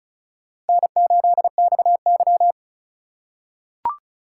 Morse code, heard as a keyed tone, about 36 wpm, 700 Hz. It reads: N8XY
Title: USA Call Signs - 4 Characters - 8x Character Spacing - 35wpm